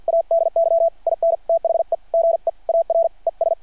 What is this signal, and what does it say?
Morse code, heard as a keyed tone, about 53 wpm, spaced as for 44 wpm, 670 Hz. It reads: ADY IN THE GEAR ES
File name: archived / CRSnet